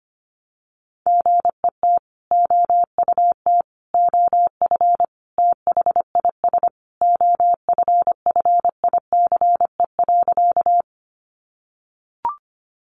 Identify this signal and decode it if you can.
Morse code, heard as a keyed tone, about 25 wpm, 700 Hz. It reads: GET OUT OF THIS OFFICE.
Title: Sentences from Top 1000 Words 25wpm